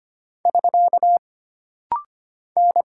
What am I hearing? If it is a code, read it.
Morse code, heard as a keyed tone, about 25 wpm, 700 Hz. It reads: <SK> N
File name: Single Character (Letter:Number:Punctuation:Prosign) - Rapid-Fire - 25wpm